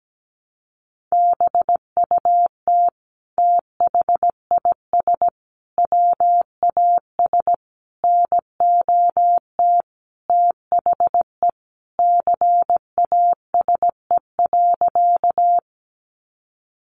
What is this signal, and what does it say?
Morse code, heard as a keyed tone, about 17 wpm, 700 Hz. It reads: BUT THIS WAS NOT THE CASE.
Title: Sentences from Top 700 Words 17wpm